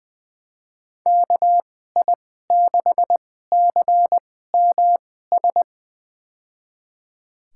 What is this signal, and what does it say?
Morse code, heard as a keyed tone, about 20 wpm, spaced as for 14 wpm, 700 Hz. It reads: KI6CMS
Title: USA Call Signs - 6 Characters - 2x Character Spacing - 20wpm